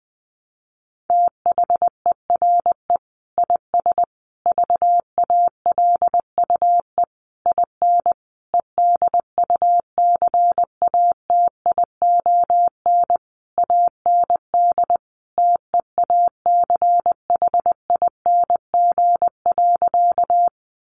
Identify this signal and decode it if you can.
Morse code, heard as a keyed tone, about 20 wpm, 700 Hz. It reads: THERE IS VALUE IN EDUCATION AND TEACHING.
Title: Sentences from Top 1100 Words 20wpm